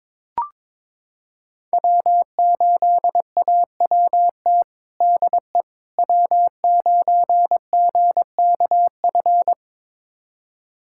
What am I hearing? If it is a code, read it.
Morse code, heard as a keyed tone, about 22 wpm, 700 Hz. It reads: W8AWT DE W9GKF
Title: USA Callsign DE Callsign - 5 Characters - 22wpm